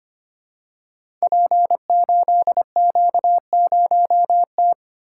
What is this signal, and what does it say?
Morse code, heard as a keyed tone, about 25 wpm, 700 Hz. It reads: P8Q0T
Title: Code Groups - 5 Letter:Numbers - 25wpm